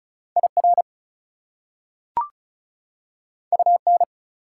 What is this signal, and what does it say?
Morse code, heard as a keyed tone, about 36 wpm, 700 Hz. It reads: IR UN
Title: Prefixes and Suffixes 35wpm